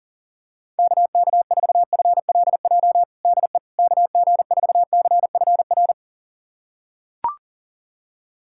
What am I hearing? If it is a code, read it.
Morse code, heard as a keyed tone, about 40 wpm, 690 Hz. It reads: KK4FLJ DE KG4CFR
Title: USA Callsign DE Callsign - 6 Characters - 40wpm